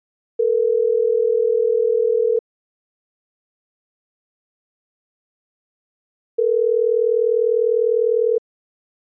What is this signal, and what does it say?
Ringback tone (North America), 440 + 480 Hz.
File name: Clean Phone Tones 24371